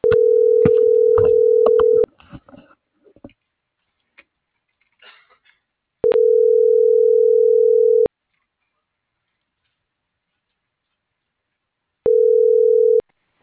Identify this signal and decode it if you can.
Ringback tone (North America), 440 + 480 Hz.